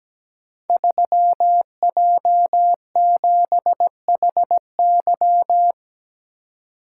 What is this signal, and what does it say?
Morse code, heard as a keyed tone, about 17 wpm, 700 Hz. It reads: 3J7HY
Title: Code Groups - 5 Letter:Numbers - 17wpm